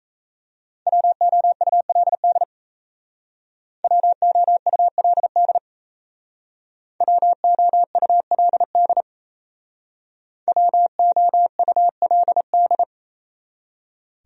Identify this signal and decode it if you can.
Morse code, heard as a keyed tone, about 35 wpm, 700 Hz. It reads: WOULD WOULD WOULD WOULD